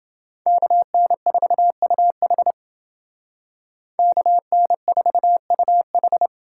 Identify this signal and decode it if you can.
Morse code, heard as a keyed tone, about 28 wpm, 700 Hz. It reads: KN4UH KN4UH